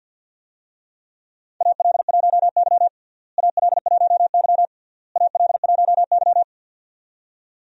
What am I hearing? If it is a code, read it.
Morse code, heard as a keyed tone, about 51 wpm, 700 Hz. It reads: AL1Y AL1Y AL1Y
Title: USA Call Signs - 4 Characters - Repeated 3x - 50wpm